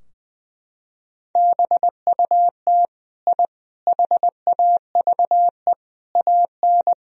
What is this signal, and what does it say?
Morse code, heard as a keyed tone, about 21 wpm, 700 Hz. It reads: BUT I HAVE AN
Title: Sentences from Top 1000 Words - Review Part 3 - 20wpm